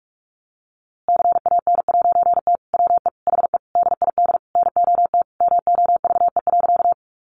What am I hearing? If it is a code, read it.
Morse code, heard as a keyed tone, about 45 wpm, 700 Hz. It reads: CAN'T WE HE DID NOT MOVE.
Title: A Sound Of Thunder 45wpm